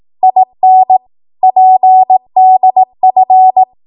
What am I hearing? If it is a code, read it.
Morse code, heard as a keyed tone, about 18 wpm, 750 Hz. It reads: IN PDF
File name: Morse / 18WPM